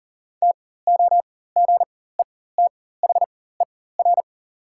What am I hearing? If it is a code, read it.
Morse code, heard as a keyed tone, about 40 wpm, spaced as for 19 wpm, 700 Hz. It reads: TOGETHER
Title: 4x Character Spacing - Top 300 Words - 40wpm